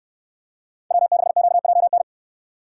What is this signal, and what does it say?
Morse code, heard as a keyed tone, about 64 wpm, 700 Hz. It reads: W6QJN